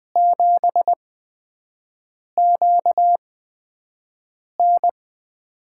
Morse code, heard as a keyed tone, about 20 wpm, spaced as for 5 wpm, 700 Hz. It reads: 7QN
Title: USA Call Signs - 8x Character Spacing - 20wpm